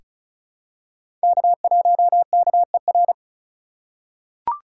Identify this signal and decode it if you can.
Morse code, heard as a keyed tone, about 36 wpm, 700 Hz. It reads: K1KER
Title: USA Call Signs - 5 Characters - 2x Character Spacing - 35wpm